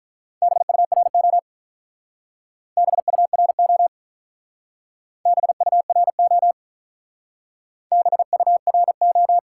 Morse code, heard as a keyed tone, about 43 wpm, 700 Hz. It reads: BURO BURO BURO BURO
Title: Speed-Racing Top 180 QSO Elements 35wpm